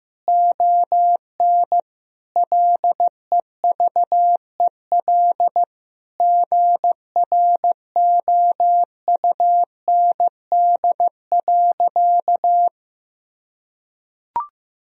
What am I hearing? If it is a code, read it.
Morse code, heard as a keyed tone, about 15 wpm, 700 Hz. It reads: ON LEVEL GROUND.
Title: Sentences from Top 1000 Words - Review Part 4 - 15wpm